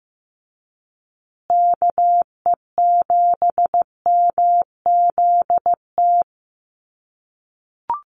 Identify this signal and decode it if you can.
Morse code, heard as a keyed tone, about 15 wpm, 700 Hz. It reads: KE7MZT
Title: USA Call Signs Repeated 2x 15wpm